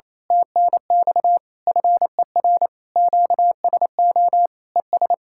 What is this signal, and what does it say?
Morse code, heard as a keyed tone, about 27 wpm, 700 Hz. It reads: TNX FER QSO ES
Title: Speed-Racing Common QSO Phrases - 28wpm